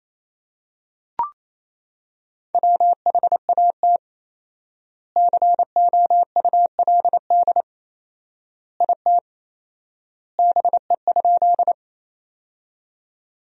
Morse code, heard as a keyed tone, about 28 wpm, 700 Hz. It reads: WHAT COULD IT BE?
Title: 4x Word-Spacing - Sentences from Top 100 Words - 28wpm